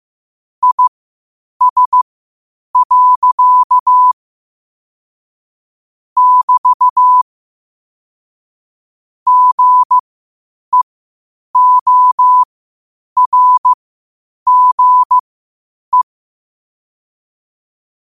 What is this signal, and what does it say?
Morse code, heard as a keyed tone, about 15 wpm, spaced as for 8 wpm, 1000 Hz. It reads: IS. <BT> GEORGE
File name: qotdfiles / 8wpm